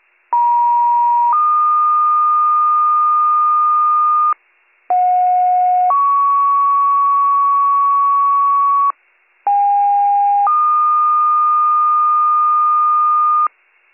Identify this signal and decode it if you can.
Two-tone page, heard as a one-second tone, then a three-second tone, about 950 then 1220 Hz, about 730 then 1120 Hz, about 810 then 1220 Hz. It is heard from 0.3 to 13.5 s.